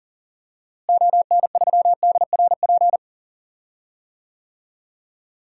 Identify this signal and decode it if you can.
Morse code, heard as a keyed tone, about 39 wpm, 700 Hz. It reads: ON3DRP
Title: DXCC-100 Call Signs - 6 Characters - Repeated 2x - 40wpm